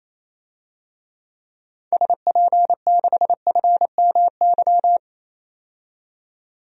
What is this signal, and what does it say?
Morse code, heard as a keyed tone, about 28 wpm, 700 Hz. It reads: SP6FMY
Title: DXCC-100 Call Signs - 6 Characters - 28wpm